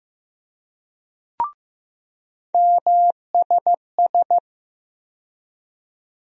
Morse code, heard as a keyed tone, about 15 wpm, 700 Hz. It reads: MSS
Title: USA Call Sign Postfixes - 3 Letters 15wpm